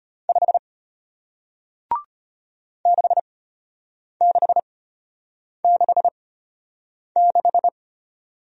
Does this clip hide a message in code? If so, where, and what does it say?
Yes: Morse code from 0.3 to 7.7 s, heard as a keyed tone, about 34 wpm, spaced as for 8 wpm, 690 Hz. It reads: 5 6666